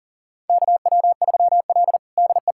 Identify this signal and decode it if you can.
Morse code, heard as a keyed tone, about 40 wpm, 700 Hz. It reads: KW3L DE
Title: USA Callsign DE Callsign - 4 Characters - 40wpm